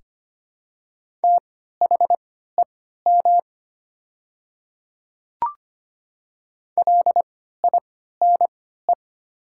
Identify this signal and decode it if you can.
Morse code, heard as a keyed tone, about 25 wpm, spaced as for 14 wpm, 700 Hz. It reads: THEM LINE
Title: 3x Character Spacing - Top 200 Words - 25wpm